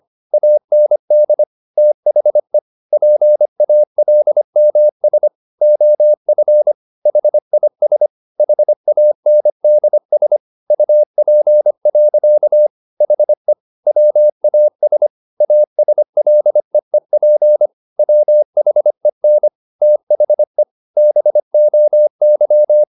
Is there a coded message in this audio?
Yes: Morse code from 0.3 to 22.8 s, heard as a keyed tone, about 25 wpm, 600 Hz. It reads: AND THE PALMS OF HIS HANDS UP. HE WAS ASLEEP WHEN THE BOY